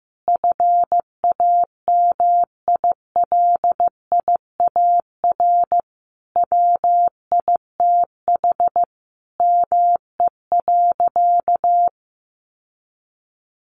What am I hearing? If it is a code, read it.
Morse code, heard as a keyed tone, about 14 wpm, 700 Hz. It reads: FAMILIAR WITH ME.
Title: Sentences from Top 1000 Words 15wpm